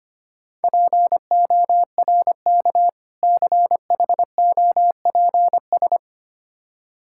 Morse code, heard as a keyed tone, about 25 wpm, 700 Hz. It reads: PORK CHOPS
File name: practice / 25wpm / Difficult Words with Hints - 25wpm